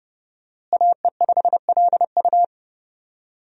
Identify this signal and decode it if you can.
Morse code, heard as a keyed tone, about 30 wpm, 700 Hz. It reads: AE5LU
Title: USA Call Signs - 5 Characters - Repeated 3x - 30wpm